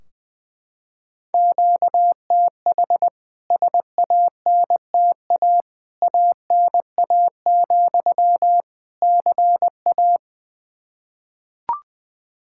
Morse code, heard as a keyed tone, about 20 wpm, 700 Hz. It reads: QTH SANTA ANA, CA
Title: K1USN SST QSO Practice 20wpm